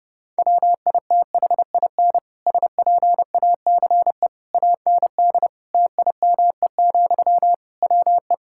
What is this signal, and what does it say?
Morse code, heard as a keyed tone, about 30 wpm, 700 Hz. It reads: WITHIN SPACE AND TIME, WE
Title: Sentences from Top 1000 Words - Review Part 5 - 30wpm